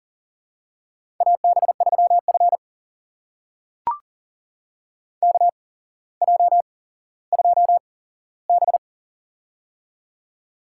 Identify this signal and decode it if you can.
Morse code, heard as a keyed tone, about 42 wpm, spaced as for 17 wpm, 700 Hz. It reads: AB3F KJ2B
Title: USA Call Signs - 4 Characters - 8x Character Spacing - 40wpm